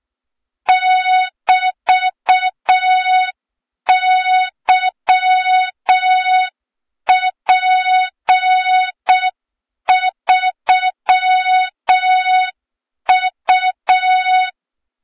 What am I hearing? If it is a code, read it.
Morse code, heard as a keyed tone, about 6 wpm, 3700 Hz. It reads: <BT>YP3U
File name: Morze-HA2MN / hang / 23 GYAKORLAT